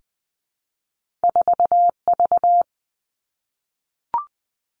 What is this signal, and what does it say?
Morse code, heard as a keyed tone, about 20 wpm, 700 Hz. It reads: 4V
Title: Two Challenging Letter-Number Combinations 20wpm